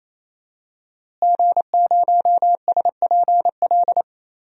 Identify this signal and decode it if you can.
Morse code, heard as a keyed tone, about 28 wpm, 700 Hz. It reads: G0SPL
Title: DXCC-100 Call Signs - 5 Characters - Repeated 3x - 28wpm